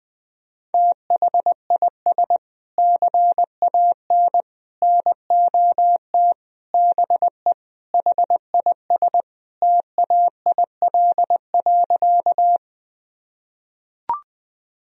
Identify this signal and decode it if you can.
Morse code, heard as a keyed tone, about 20 wpm, 700 Hz. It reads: THIS CAN NOT BE HIS TAIL.
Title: Sentences from Top 1000 Words - Review Part 3 - 20wpm